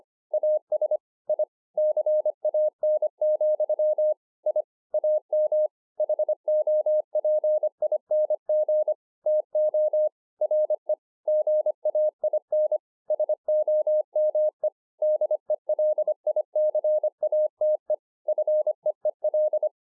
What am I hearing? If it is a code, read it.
Morse code, heard as a keyed tone, about 25 wpm, 610 Hz. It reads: AS I CAN, I AM HOPING TO RE GAIN SOME DELICATE FEEL